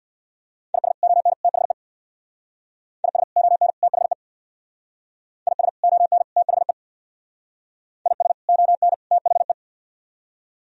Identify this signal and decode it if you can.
Morse code, heard as a keyed tone, about 62 wpm, 700 Hz. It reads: IS ON THE IS ON THE IS ON THE IS ON THE
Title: Speed-Racing - Sets of 3 Words 50wpm